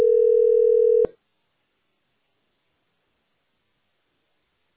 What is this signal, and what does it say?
Ringback tone (North America), 440 + 480 Hz.